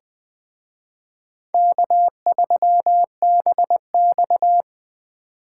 Morse code, heard as a keyed tone, about 21 wpm, 700 Hz. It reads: K3BX